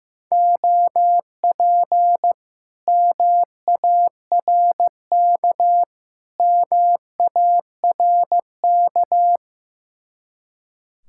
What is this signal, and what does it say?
Morse code, heard as a keyed tone, about 15 wpm, 700 Hz. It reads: OP MARK MARK
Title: Speed-Racing Common QSO Phrases Encore - 15wpm